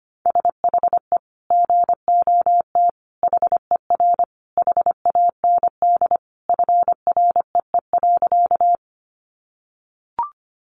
Morse code, heard as a keyed tone, about 25 wpm, 700 Hz. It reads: SHE GOT HER HAND FREE.
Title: Sentences from Top 500 Words 25wpm